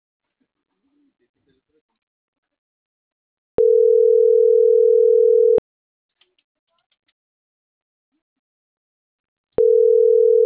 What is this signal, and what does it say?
Ringback tone (North America), 440 + 480 Hz.